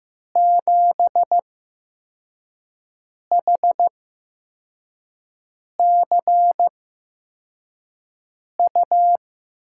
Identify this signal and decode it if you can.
Morse code, heard as a keyed tone, about 15 wpm, spaced as for 4 wpm, 700 Hz. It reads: 7HCU
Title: USA Call Signs - 6 Characters - 8x Character Spacing - 15wpm